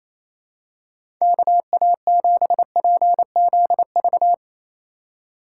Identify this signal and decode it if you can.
Morse code, heard as a keyed tone, about 28 wpm, 690 Hz. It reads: KA7PZV
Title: USA Call Signs - 6 Characters - 2x Character Spacing - 28wpm